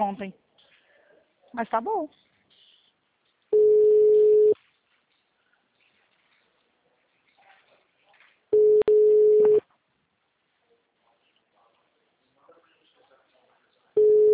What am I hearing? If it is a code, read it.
Ringback tone (Europe, 425 Hz), with speech.